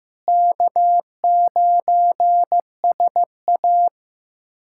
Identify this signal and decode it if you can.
Morse code, heard as a keyed tone, about 16 wpm, 700 Hz. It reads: K9SA
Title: USA Callsign DE Callsign - 4 Characters - 15wpm